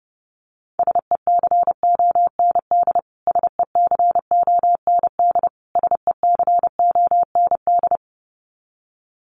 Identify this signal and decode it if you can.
Morse code, heard as a keyed tone, about 30 wpm, 700 Hz. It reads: SECOND SECOND SECOND
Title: Top 300 Words Repeated 30wpm